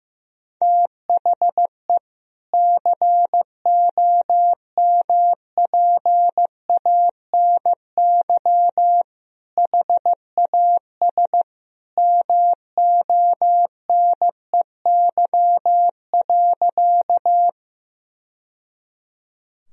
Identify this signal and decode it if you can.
Morse code, heard as a keyed tone, about 15 wpm, 700 Hz. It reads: THE COMPANY HAS MONEY.
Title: Sentences from Top 1000 Words - Review Part 3 - 15wpm